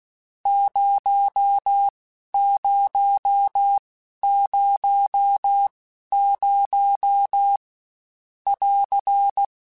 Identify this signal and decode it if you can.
Morse code, heard as a keyed tone, about 16 wpm, spaced as for 11 wpm, 800 Hz. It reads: 0000 <AR>
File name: Misc / Misc-mp3s / Numb @15